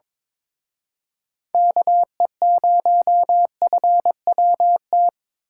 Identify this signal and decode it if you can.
Morse code, heard as a keyed tone, about 26 wpm, 700 Hz. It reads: KE0FWT
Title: Speed Racing - USA 2x3 Call Signs 22wpm